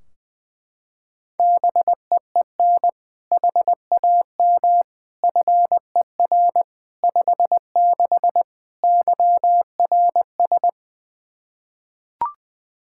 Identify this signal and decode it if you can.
Morse code, heard as a keyed tone, about 20 wpm, 700 Hz. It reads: BEEN HAM FER 56 YRS
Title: Practice QSO - Exchange 3 - More Info - 20wpm